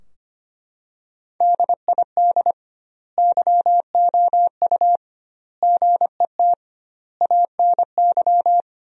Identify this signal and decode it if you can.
Morse code, heard as a keyed tone, about 25 wpm, 700 Hz. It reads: DID YOU GET ANY
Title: 2x Word-Spacing - Sentences from Top 400 Words - 25wpm